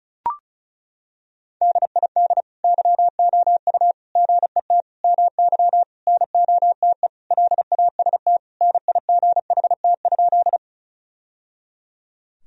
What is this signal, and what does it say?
Morse code, heard as a keyed tone, about 34 wpm, 700 Hz. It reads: DID YOU GET MY NOTE LAST NIGHT?
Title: Sentences from Top 1000 Words - Review Part 2 - 35wpm